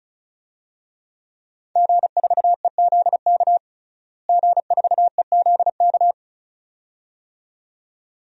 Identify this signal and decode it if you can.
Morse code, heard as a keyed tone, about 36 wpm, 700 Hz. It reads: G4EZK G4EZK
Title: DXCC-100 Call Signs - 5 Characters - Repeated 2x - 35wpm